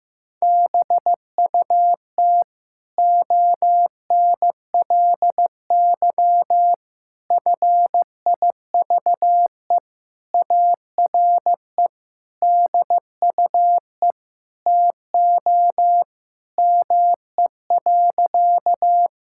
Morse code, heard as a keyed tone, about 15 wpm, 700 Hz. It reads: BUT ONLY FIVE ARE DUE TO ME.
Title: Sentences from Top 1000 Words - Review Part 6 - 15wpm